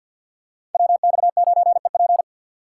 Morse code, heard as a keyed tone, about 47 wpm, 700 Hz. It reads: WX9EP
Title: USA Call Signs - 5 Characters - 6x Character Spacing - 50wpm